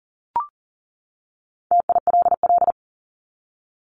Morse code, heard as a keyed tone, about 40 wpm, 700 Hz. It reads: TILL
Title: Top 1751-2000 Words 40wpm